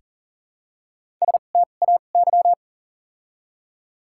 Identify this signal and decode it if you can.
Morse code, heard as a keyed tone, about 40 wpm, 700 Hz. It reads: S T A Y
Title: 2x Character Spacing - Top 500 Words - 40wpm